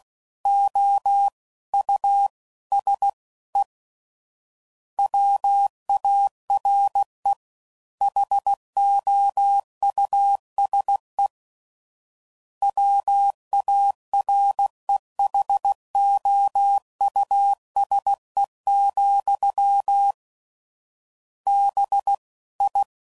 Morse code, heard as a keyed tone, about 16 wpm, 800 Hz. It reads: O U S E WARE HOUSE WAREHOUSE, B I
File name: Suffixes / General / Houses